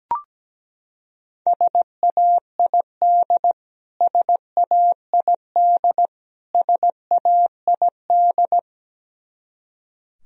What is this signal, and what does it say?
Morse code, heard as a keyed tone, about 17 wpm, 700 Hz. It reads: SAID SAID SAID
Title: Top 100 Words Repeated 17wpm